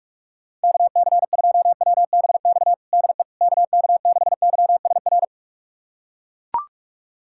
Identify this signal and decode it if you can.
Morse code, heard as a keyed tone, about 41 wpm, 700 Hz. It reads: KC2WBX DE KK6YSR
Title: USA Callsign DE Callsign - 6 Characters - 45wpm